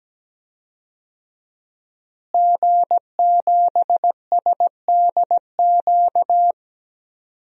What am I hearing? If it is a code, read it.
Morse code, heard as a keyed tone, about 17 wpm, 700 Hz. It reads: G7SDQ
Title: DXCC-100 Call Signs - 5 Characters - Repeated 3x - 17wpm